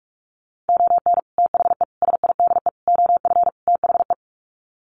Morse code, heard as a keyed tone, about 44 wpm, 700 Hz. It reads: ON THE SIDE OF THE
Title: Sets of 5 Words 45wpm